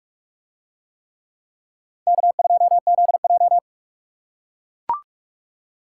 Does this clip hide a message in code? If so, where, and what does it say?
Yes: Morse code from 2.1 to 3.6 s, heard as a keyed tone, about 44 wpm, 700 Hz. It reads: K2ZJ